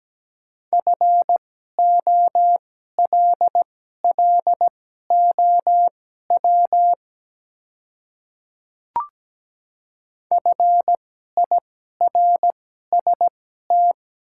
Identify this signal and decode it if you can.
Morse code, heard as a keyed tone, about 17 wpm, spaced as for 12 wpm, 700 Hz. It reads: FOLLOW FIRST
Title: 2x Character Spacing - Top 200 Words - 17wpm